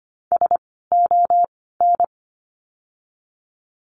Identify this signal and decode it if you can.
Morse code, heard as a keyed tone, about 25 wpm, spaced as for 16 wpm, 700 Hz. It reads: SON
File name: course / 25:10wpm / 094 Lesson 16f - Letter c - Words 25wpm